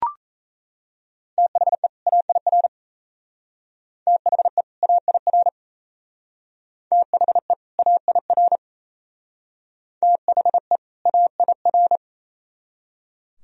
Morse code, heard as a keyed tone, about 35 wpm, 700 Hz. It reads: THE AIR THE AIR THE AIR THE AIR
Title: Speed-Racing - Sets of 2 Words 28wpm